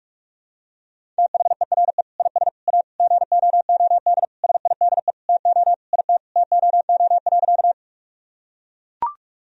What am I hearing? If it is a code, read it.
Morse code, heard as a keyed tone, about 42 wpm, 700 Hz. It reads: THERE IS A GOOD SIDE TO IT TOO.
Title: Sentences from Top 1000 Words - Review Part 1 - 45wpm